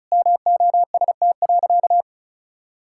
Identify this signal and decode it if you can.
Morse code, heard as a keyed tone, about 36 wpm, 700 Hz. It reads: MOST.
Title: Sentences from Top 1600 Words 35wpm